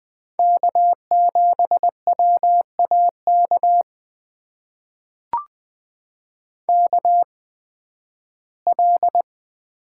Morse code, heard as a keyed tone, about 20 wpm, spaced as for 13 wpm, 700 Hz. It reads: K7WAK KL
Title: USA Call Signs - 5 Characters - 8x Character Spacing - 20wpm